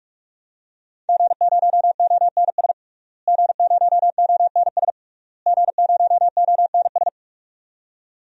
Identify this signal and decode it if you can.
Morse code, heard as a keyed tone, about 46 wpm, 700 Hz. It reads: G0ONS G0ONS G0ONS
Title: DXCC-100 Call Signs - 5 Characters - Repeated 3x - 45wpm